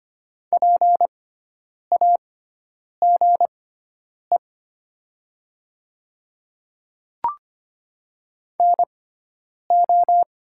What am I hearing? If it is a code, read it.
Morse code, heard as a keyed tone, about 25 wpm, spaced as for 9 wpm, 700 Hz. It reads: PAGE NO
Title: 6x Character Spacing - Top 500 Words - 25wpm